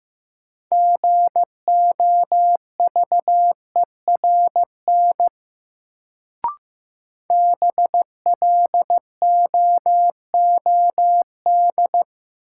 Morse code, heard as a keyed tone, about 15 wpm, 700 Hz. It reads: GOVERN BLOOD
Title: Sentences from Top 1000 Words - Review Part 2 - New Word Drill - 15wpm